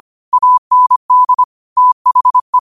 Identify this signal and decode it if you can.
Morse code, heard as a keyed tone, about 25 wpm, 1000 Hz. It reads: AND THE
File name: qotdfiles / 25wpm